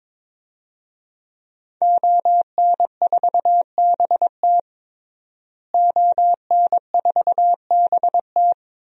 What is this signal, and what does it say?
Morse code, heard as a keyed tone, about 22 wpm, 700 Hz. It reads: ON4BT ON4BT
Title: DXCC-100 Call Signs - 5 Characters - Repeated 2x - 22wpm